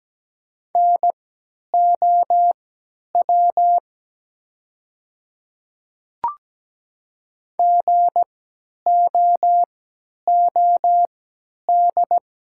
Morse code, heard as a keyed tone, about 17 wpm, spaced as for 10 wpm, 700 Hz. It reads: NOW GOOD
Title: 3x Character Spacing - Top 300 Words - 17wpm